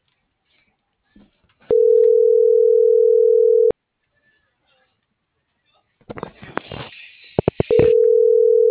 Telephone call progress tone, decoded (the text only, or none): ringback tone